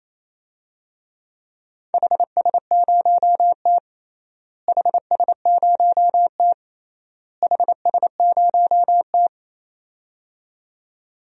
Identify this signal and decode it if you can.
Morse code, heard as a keyed tone, about 28 wpm, 700 Hz. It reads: HS0T HS0T HS0T
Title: DXCC-100 Call Signs - 4 Characters - Repeated 3x - 28wpm